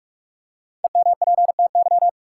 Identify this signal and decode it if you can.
Morse code, heard as a keyed tone, about 46 wpm, 700 Hz. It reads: EMPTY
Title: Five Letter Words - 45wpm